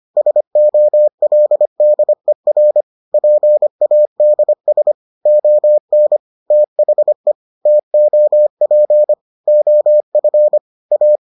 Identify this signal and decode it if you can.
Morse code, heard as a keyed tone, about 25 wpm, 600 Hz. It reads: SOLDER PADS ON THE TOP OF A